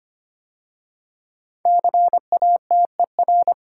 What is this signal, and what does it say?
Morse code, heard as a keyed tone, about 25 wpm, 700 Hz. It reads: CATER
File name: 5 Letter Words - First 3 Letters Given - 25wpm